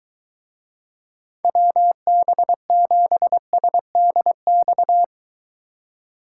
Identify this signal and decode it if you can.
Morse code, heard as a keyed tone, about 23 wpm, 700 Hz. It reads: WB7SDX